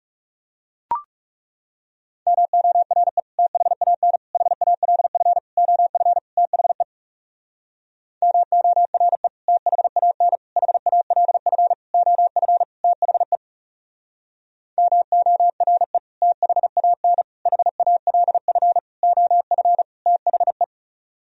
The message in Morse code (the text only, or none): MORE THAN HALF OF THE MORE THAN HALF OF THE MORE THAN HALF OF THE